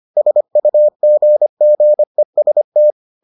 Morse code, heard as a keyed tone, about 25 wpm, 600 Hz. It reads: SUGGEST